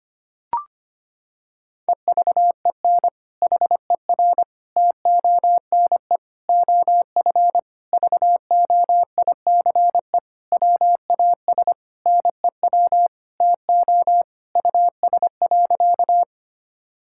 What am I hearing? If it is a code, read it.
Morse code, heard as a keyed tone, about 25 wpm, 700 Hz. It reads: EVEN HER TONE OF VOICE WAS NEW TO US.
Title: Sentences from Top 1600 Words 25wpm